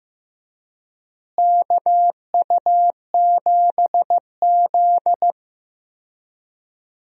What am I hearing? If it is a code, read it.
Morse code, heard as a keyed tone, about 15 wpm, 700 Hz. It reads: KU7Z
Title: USA Call Signs - 4 Characters - Repeated 3x - 15wpm